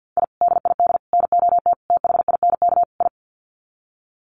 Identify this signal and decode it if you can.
Morse code, heard as a keyed tone, about 50 wpm, 700 Hz. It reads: I DID NOT THINK I
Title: Sets of 5 Words Encore 50wpm